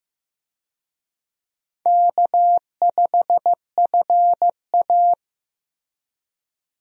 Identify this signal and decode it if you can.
Morse code, heard as a keyed tone, about 15 wpm, 700 Hz. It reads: K5FA
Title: USA Call Signs - 4 Characters - Repeated 2x - 15wpm